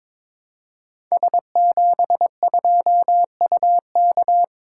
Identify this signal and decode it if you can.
Morse code, heard as a keyed tone, about 22 wpm, 700 Hz. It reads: S72UK
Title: Code Groups - 5 Letter:Numbers - 22wpm